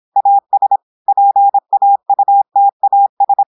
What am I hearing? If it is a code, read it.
Morse code, heard as a keyed tone, about 26 wpm, 800 Hz. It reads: AS PAUTAS
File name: portugais texte en clair 2 vitesse 26 mots minutes